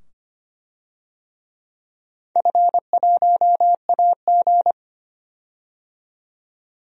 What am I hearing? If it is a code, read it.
Morse code, heard as a keyed tone, about 25 wpm, 700 Hz. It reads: F1AG
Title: DXCC-100 Call Signs - 4 Characters - 25wpm